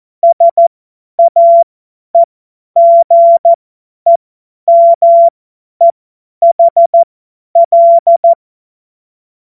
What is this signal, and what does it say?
Morse code, heard as a keyed tone, about 14 wpm, spaced as for 10 wpm, 680 Hz. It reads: SAEGEMEHL
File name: Morsekurs / Witze / Witz 18. Mai 2021, 50 F70